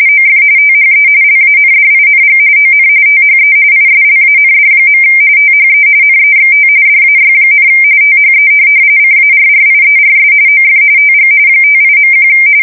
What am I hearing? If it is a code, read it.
RTTY, about 45.45 baud. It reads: OF THE LANDING ON THE ISLAND. THE NEXT DAY, THE CARRIERS OF TG 58.1
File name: fsk170 / USS-Massachusetts